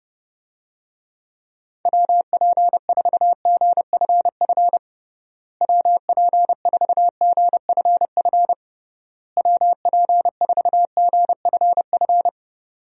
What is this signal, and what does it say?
Morse code, heard as a keyed tone, about 30 wpm, 700 Hz. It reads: WP4GFF WP4GFF WP4GFF
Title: USA Call Signs - 6 Characters - Repeated 3x - 30wpm